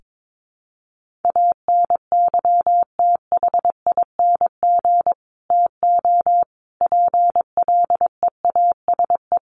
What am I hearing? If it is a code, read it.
Morse code, heard as a keyed tone, about 22 wpm, 700 Hz. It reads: ANYTHING TO PLEASE
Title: Sentences from Top 1500 Words 22wpm